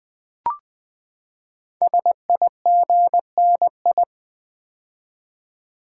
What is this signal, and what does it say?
Morse code, heard as a keyed tone, about 20 wpm, 700 Hz. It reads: SIGNI
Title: Build A Word - 5 Letters At A Time - 12-Letter Words - 20wpm